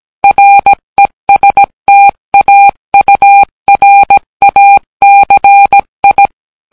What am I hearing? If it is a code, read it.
Morse code, heard as a keyed tone, about 17 wpm, 800 Hz. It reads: RESTAURACI